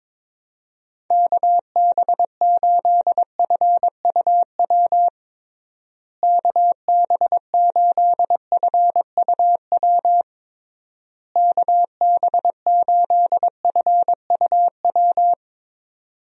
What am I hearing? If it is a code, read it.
Morse code, heard as a keyed tone, about 22 wpm, 700 Hz. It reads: KB8FUW KB8FUW KB8FUW
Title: USA Call Signs - 6 Characters - Repeated 3x - 22wpm